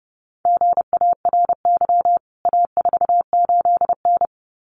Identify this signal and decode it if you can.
Morse code, heard as a keyed tone, about 28 wpm, 700 Hz. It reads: GARY A48N
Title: CWT Exchange 30wpm